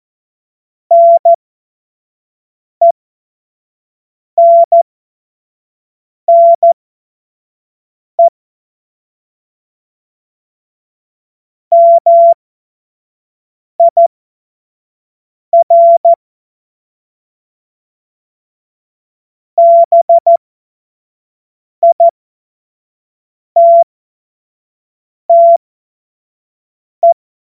Morse code, heard as a keyed tone, about 14 wpm, spaced as for 5 wpm, 680 Hz. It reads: NENNE MIR BITTE
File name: Morsekurs / Witze / Witz 27. September 2022 25 F70